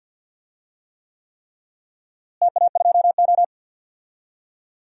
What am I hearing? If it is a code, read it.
Morse code, heard as a keyed tone, about 50 wpm, 700 Hz. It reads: TA2O
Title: DXCC-100 Call Signs - 4 Characters - Repeated 3x - 50wpm